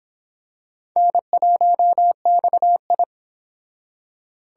Morse code, heard as a keyed tone, about 26 wpm, 700 Hz. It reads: N1XI